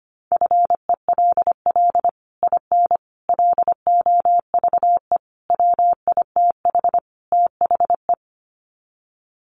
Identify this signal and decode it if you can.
Morse code, heard as a keyed tone, about 25 wpm, 700 Hz. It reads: FELL IN LOVE WITH THE
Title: Sets of 5 Words Encore2 25wpm